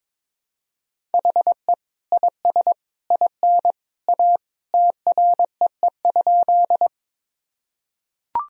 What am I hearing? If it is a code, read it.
Morse code, heard as a keyed tone, about 22 wpm, 700 Hz. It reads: HE IS IN A TREE?
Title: Sentences from Top 1000 Words - Review Part 1 - 22wpm